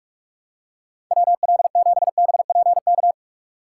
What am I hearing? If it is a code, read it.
Morse code, heard as a keyed tone, about 44 wpm, 700 Hz. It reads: WL7BPK